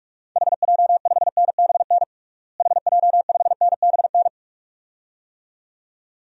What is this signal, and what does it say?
Morse code, heard as a keyed tone, about 46 wpm, 700 Hz. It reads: HJ5NBN HJ5NBN
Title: DXCC-100 Call Signs - 6 Characters - Repeated 2x - 45wpm